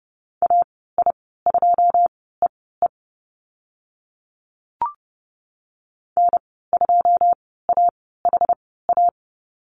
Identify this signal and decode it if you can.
Morse code, heard as a keyed tone, about 30 wpm, spaced as for 17 wpm, 700 Hz. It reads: AI2EE N2AHA
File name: Lesson 15g - Number 5 - Call Signs 30wpm